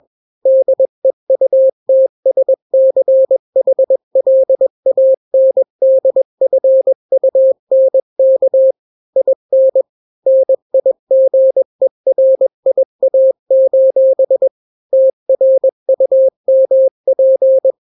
Morse code, heard as a keyed tone, about 21 wpm, 530 Hz. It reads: DEUTSCHLANDFUNK IN NIGERIA: TRUMP
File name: daily / 20wpm